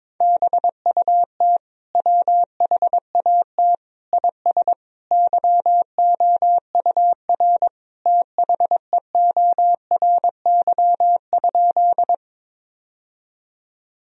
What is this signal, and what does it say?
Morse code, heard as a keyed tone, about 22 wpm, 700 Hz. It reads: BUT WHAT IS YOUR THEORY?
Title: Sentences from Top 1000 Words - Review Part 3 - 22wpm